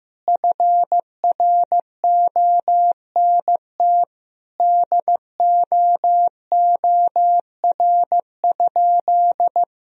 Morse code, heard as a keyed tone, about 16 wpm, 700 Hz. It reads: FRONT DOOR?
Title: Sentences from Top 1000 Words - Review Part 2 - 15wpm